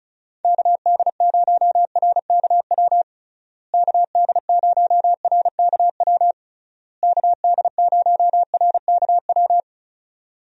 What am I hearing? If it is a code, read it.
Morse code, heard as a keyed tone, about 34 wpm, 700 Hz. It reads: KD0RKW KD0RKW KD0RKW
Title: USA Call Signs - 6 Characters - Repeated 3x - 35wpm